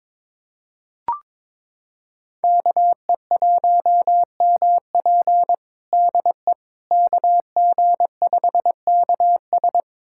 Morse code, heard as a keyed tone, about 22 wpm, 700 Hz. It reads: KE1MP DE KG5KS
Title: USA Callsign DE Callsign - 5 Characters - 22wpm